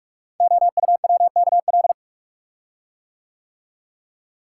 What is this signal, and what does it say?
Morse code, heard as a keyed tone, about 44 wpm, 700 Hz. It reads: OUWKL
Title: Code Groups - 5 Letters - 45wpm